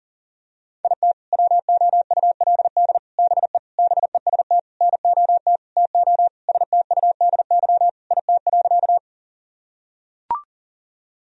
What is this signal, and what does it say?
Morse code, heard as a keyed tone, about 40 wpm, 700 Hz. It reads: IT WOULD BE BEST NOT TO STUDY IT.
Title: Sentences from Top 1000 Words - Review Part 1 - 40wpm